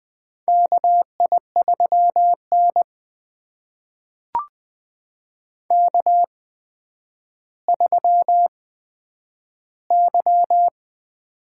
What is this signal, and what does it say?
Morse code, heard as a keyed tone, about 20 wpm, spaced as for 10 wpm, 700 Hz. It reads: KI3N K3Y
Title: USA Call Signs - 4 Characters - 8x Character Spacing - 20wpm